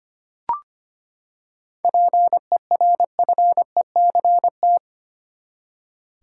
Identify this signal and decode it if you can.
Morse code, heard as a keyed tone, about 25 wpm, 700 Hz. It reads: PERFECT
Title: Seven Letter Words - 25wpm